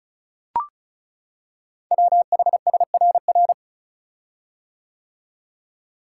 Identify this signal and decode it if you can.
Morse code, heard as a keyed tone, about 36 wpm, 690 Hz. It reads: WHSRR
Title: Code Groups - 5 Letters - 35wpm